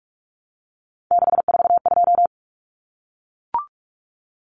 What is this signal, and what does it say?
Morse code, heard as a keyed tone, about 44 wpm, 700 Hz. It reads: 642
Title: Numbers - 3-Digit 45wpm